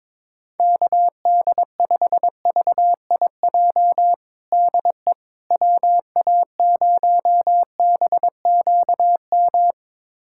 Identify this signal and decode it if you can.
Morse code, heard as a keyed tone, about 22 wpm, 700 Hz. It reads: KD5VIJ DE WA0BQM
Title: USA Callsign DE Callsign - 6 Characters - 22wpm